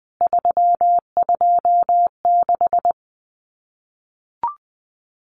Morse code, heard as a keyed tone, about 20 wpm, 700 Hz. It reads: 326
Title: CWT Exchange 20wpm